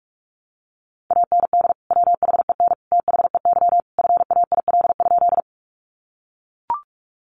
Morse code, heard as a keyed tone, about 44 wpm, 700 Hz. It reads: AND WHEN THEY FAIL?
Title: Sentences from Top 1100 Words 45wpm